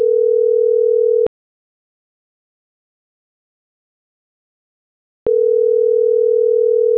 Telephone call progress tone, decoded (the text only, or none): ringback tone